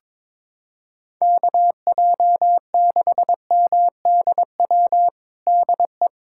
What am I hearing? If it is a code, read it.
Morse code, heard as a keyed tone, about 22 wpm, 700 Hz. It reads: KJ6MDW DE
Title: USA Callsign DE Callsign 22wpm